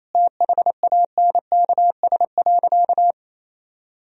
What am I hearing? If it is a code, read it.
Morse code, heard as a keyed tone, about 28 wpm, 700 Hz. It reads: THANKS.
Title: Sentences from Top 1000 Words - Review Part 3 - 28wpm